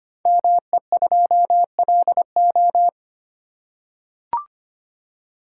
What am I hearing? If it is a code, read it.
Morse code, heard as a keyed tone, about 25 wpm, 700 Hz. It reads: ME2LO
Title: Code Groups - 5 Letter:Numbers - 25wpm